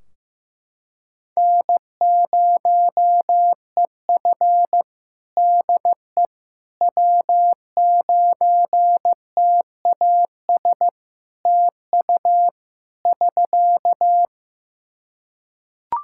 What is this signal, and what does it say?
Morse code, heard as a keyed tone, about 16 wpm, 700 Hz. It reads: N0EF DE W9TAS TU <SK>
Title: Practice QSO - Exchange 4 - Ending - 15wpm